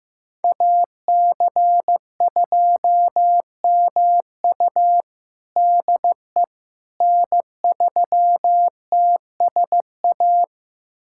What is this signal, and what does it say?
Morse code, heard as a keyed tone, about 15 wpm, 700 Hz. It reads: AC2MU DE N3TSA
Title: USA Callsign DE Callsign - 5 Characters - 15wpm